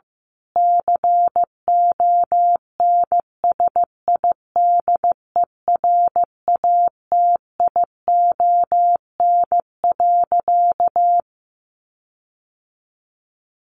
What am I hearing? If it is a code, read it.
Morse code, heard as a keyed tone, about 15 wpm, 700 Hz. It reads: CONSIDERATION.
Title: Sentences from Top 1500 Words 15wpm